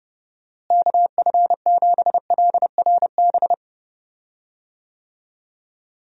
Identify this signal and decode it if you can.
Morse code, heard as a keyed tone, about 30 wpm, 700 Hz. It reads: KF7LRB
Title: USA Call Signs - 6 Characters - Repeated 2x - 30wpm